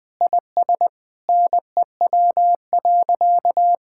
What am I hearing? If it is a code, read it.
Morse code, heard as a keyed tone, about 20 wpm, 700 Hz. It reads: IS NEW.
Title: Sentences with 2-Word Phrases - Spoken Beginning - 20wpm